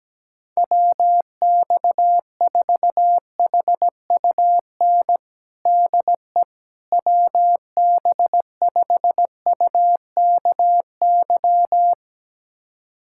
Morse code, heard as a keyed tone, about 17 wpm, 700 Hz. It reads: WX4HUN DE WB5UKY
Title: USA Callsign DE Callsign - 6 Characters - 17wpm